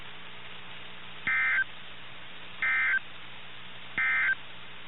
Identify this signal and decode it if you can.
Emergency Alert System (SAME) tones, heard as harsh buzzing bursts. It is heard from 1.1 to 4.4 s.
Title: NOAA Weather Radio - KIH61